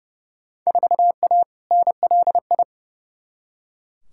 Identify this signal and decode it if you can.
Morse code, heard as a keyed tone, about 30 wpm, 700 Hz. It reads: 4A NLI
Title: Field Day Exchanges Encore - 30wpm